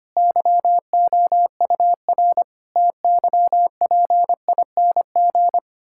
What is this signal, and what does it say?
Morse code, heard as a keyed tone, about 25 wpm, 700 Hz. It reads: YOUR TYPING
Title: Pangram Challenge v2 - 25wpm